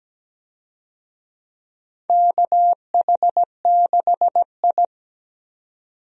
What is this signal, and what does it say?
Morse code, heard as a keyed tone, about 17 wpm, 700 Hz. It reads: KH6I
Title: DXCC-100 Call Signs - 4 Characters - 17wpm